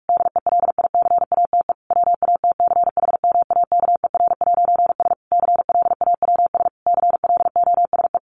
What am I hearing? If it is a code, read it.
Morse code, heard as a keyed tone, about 44 wpm, 700 Hz. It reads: DELICATE WATCHMAKER'S CLAWS CLOSE
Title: A Sound Of Thunder 45wpm